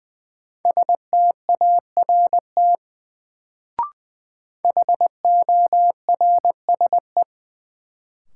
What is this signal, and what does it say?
Morse code, heard as a keyed tone, about 20 wpm, 700 Hz. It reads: START HORSE
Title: Sentences from Top 300 Words - New Word Drill - 20wpm